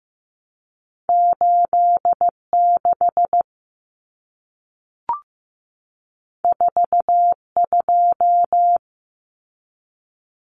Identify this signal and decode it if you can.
Morse code, heard as a keyed tone, about 15 wpm, 700 Hz. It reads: 86 42
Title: Common Cut Numbers - 2-Digit 15wpm